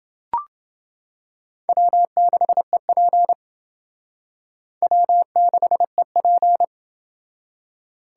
Morse code, heard as a keyed tone, about 27 wpm, 700 Hz. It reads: W6EP W6EP